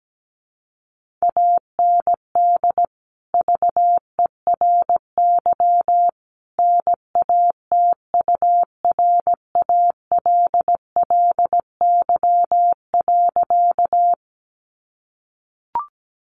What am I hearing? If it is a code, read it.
Morse code, heard as a keyed tone, about 17 wpm, 700 Hz. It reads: AND VERY NATURALLY.
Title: Sentences from Top 1300 Words 17wpm